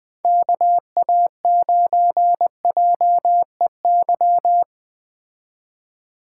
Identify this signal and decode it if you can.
Morse code, heard as a keyed tone, about 20 wpm, 700 Hz. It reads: KA9JEY
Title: Speed Racing - USA Call Signs 17wpm